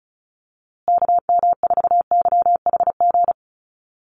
Morse code, heard as a keyed tone, about 34 wpm, 700 Hz. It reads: KM4YHG
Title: USA Call Signs Repeated 2x 35wpm